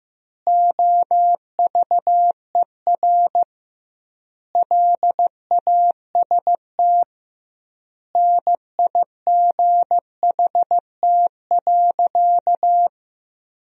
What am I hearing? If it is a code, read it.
Morse code, heard as a keyed tone, about 15 wpm, 700 Hz. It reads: OVER LAST NIGHT.
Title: 2x Word-Spacing - Sentences from Top 300 Words - 15wpm